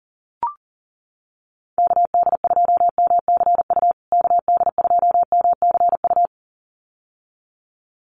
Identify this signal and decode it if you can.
Morse code, heard as a keyed tone, about 40 wpm, 700 Hz. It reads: KD2MCU KD2MCU
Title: USA Call Signs Repeated 2x 40wpm